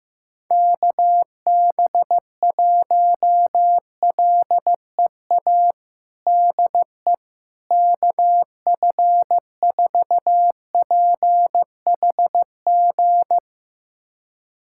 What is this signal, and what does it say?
Morse code, heard as a keyed tone, about 15 wpm, 700 Hz. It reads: KB1LEA DE KF4PHG
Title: USA Callsign DE Callsign 15wpm